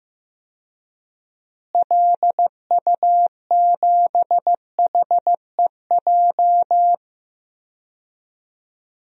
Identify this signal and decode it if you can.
Morse code, heard as a keyed tone, about 15 wpm, 700 Hz. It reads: LU7HEJ
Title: DXCC-100 Call Signs - 6 Characters - Repeated 3x - 15wpm